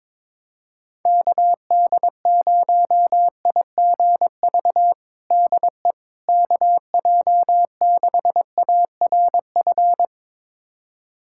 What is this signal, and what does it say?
Morse code, heard as a keyed tone, about 21 wpm, 700 Hz. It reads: KD0IGV DE KJ6ARF
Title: USA Callsign DE Callsign - 6 Characters - 22wpm